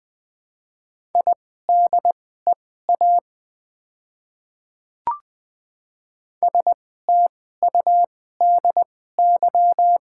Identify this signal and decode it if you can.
Morse code, heard as a keyed tone, about 20 wpm, spaced as for 14 wpm, 700 Hz. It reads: IDEA STUDY
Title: 2x Character Spacing - Top 500 Words - 20wpm